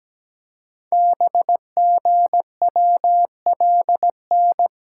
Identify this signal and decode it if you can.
Morse code, heard as a keyed tone, about 17 wpm, 700 Hz. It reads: BGWLN
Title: Code Groups - 5 Letters - 17wpm